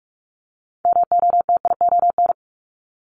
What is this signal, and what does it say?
Morse code, heard as a keyed tone, about 46 wpm, 700 Hz. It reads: MOTION
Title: Multi-letter Phonemes 45wpm